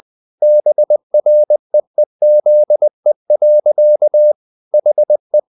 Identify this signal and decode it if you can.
Morse code, heard as a keyed tone, about 20 wpm, 600 Hz. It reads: BREEZE. HE
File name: the old man and the sea part 2 20wpm